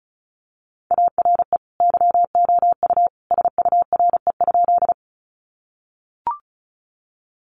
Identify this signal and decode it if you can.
Morse code, heard as a keyed tone, about 34 wpm, 700 Hz. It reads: ARE YOU SURE?
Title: Sentences from Top 300 Words 35wpm